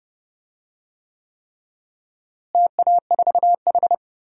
Morse code, heard as a keyed tone, about 32 wpm, 700 Hz. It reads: TA4H
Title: DXCC-100 Call Signs - 4 Characters - Repeated 3x - 30wpm